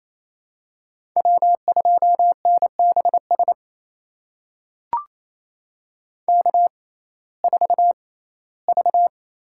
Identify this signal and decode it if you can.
Morse code, heard as a keyed tone, about 28 wpm, spaced as for 17 wpm, 700 Hz. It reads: W2NBS K4V
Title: USA Call Signs - 5 Characters - 6x Character Spacing - 28wpm